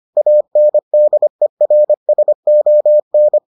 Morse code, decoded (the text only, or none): ANDERSON